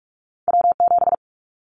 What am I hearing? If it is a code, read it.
Morse code, heard as a keyed tone, about 46 wpm, 700 Hz. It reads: W7
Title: Callsign Prefixes for Top 25 DXCC - 45wpm